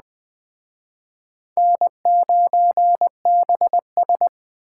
Morse code, heard as a keyed tone, about 20 wpm, 700 Hz. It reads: N9BS
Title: Speed Racing - USA 1x2 Call Signs 17wpm